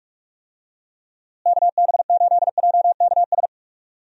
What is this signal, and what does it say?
Morse code, heard as a keyed tone, about 46 wpm, 700 Hz. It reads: KB8JKS
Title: USA Call Signs - 6 Characters - Repeated 2x - 45wpm